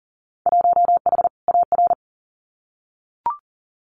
Morse code, heard as a keyed tone, about 42 wpm, 700 Hz. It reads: 1H AR
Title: Winter Field Day Exchanges 40wpm